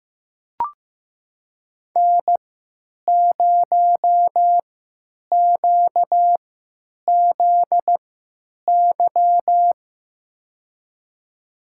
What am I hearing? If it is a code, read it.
Morse code, heard as a keyed tone, about 15 wpm, spaced as for 9 wpm, 700 Hz. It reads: N0QZY
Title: USA Call Signs - 5 Characters - 3x Character Spacing - 15wpm